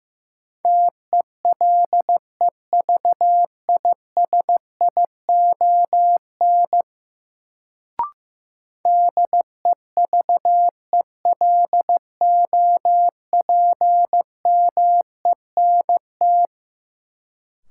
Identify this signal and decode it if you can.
Morse code, heard as a keyed tone, about 15 wpm, 700 Hz. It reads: TELEVISION DEVELOPMENT
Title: Sentences from Top 1000 Words - Review Part 4 - New Word Drill - 15wpm